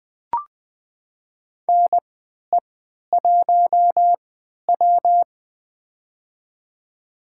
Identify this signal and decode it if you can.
Morse code, heard as a keyed tone, about 20 wpm, spaced as for 11 wpm, 700 Hz. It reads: NE1W
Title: USA Call Signs - 4 Characters - 3x Character Spacing - 20wpm